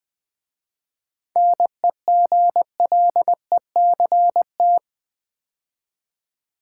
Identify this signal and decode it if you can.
Morse code, heard as a keyed tone, about 20 wpm, 690 Hz. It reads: NEGLECT
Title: 7 Letter Words - First 3 Letters Given - 20wpm